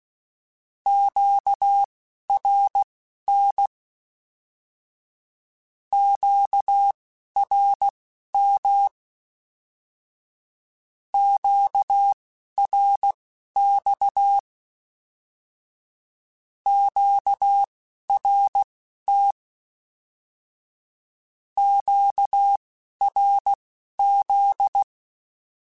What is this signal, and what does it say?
Morse code, heard as a keyed tone, about 16 wpm, spaced as for 11 wpm, 800 Hz. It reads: QRN QRM QRX QRT QRZ
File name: Q Calls / Q Calls-mp3s / Q Calls Once @10